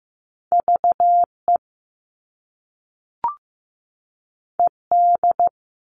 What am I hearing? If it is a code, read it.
Morse code, heard as a keyed tone, about 15 wpm, 700 Hz. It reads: VE ED
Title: All Cut Numbers - 2-Digit 15wpm